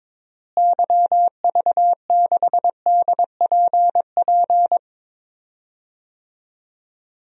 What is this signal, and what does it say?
Morse code, heard as a keyed tone, about 22 wpm, 700 Hz. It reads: YV6DPP
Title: DXCC-100 Call Signs - 6 Characters - Repeated 2x - 22wpm